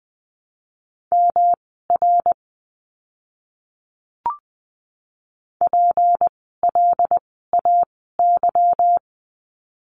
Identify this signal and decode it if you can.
Morse code, heard as a keyed tone, about 20 wpm, spaced as for 14 wpm, 700 Hz. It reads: MR PLAY
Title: 237 Lesson 38f - Letter q - Words 20wpm